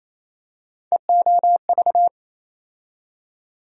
Morse code, heard as a keyed tone, about 28 wpm, 700 Hz. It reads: EOV
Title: USA Call Sign Postfixes - 3 Letters 28wpm